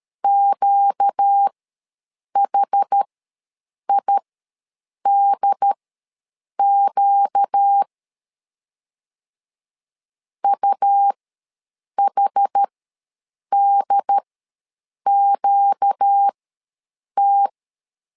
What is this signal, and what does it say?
Morse code, heard as a keyed tone, about 13 wpm, spaced as for 7 wpm, 800 Hz. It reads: QHIDQ UHDQT
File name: LEZIONI / 4 Q ws 12wpm 1912